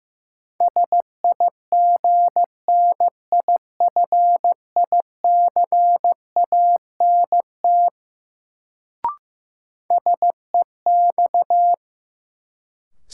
Morse code, heard as a keyed tone, about 15 wpm, 700 Hz. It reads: SIGNIFICANT SEX
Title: Sentences from Top 1600 Words - New Word Drill Alt - 15wpm